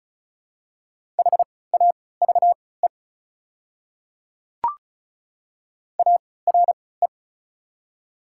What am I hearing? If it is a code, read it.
Morse code, heard as a keyed tone, about 34 wpm, spaced as for 20 wpm, 700 Hz. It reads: HAVE ARE
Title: 3x Character Spacing - Top 200 Words - 35wpm